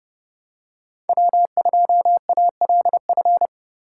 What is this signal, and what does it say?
Morse code, heard as a keyed tone, about 29 wpm, 700 Hz. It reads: W2ALF